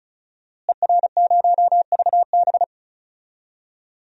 Morse code, heard as a keyed tone, about 34 wpm, 700 Hz. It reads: ER0VB
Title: DXCC-100 Call Signs - 5 Characters - Repeated 3x - 35wpm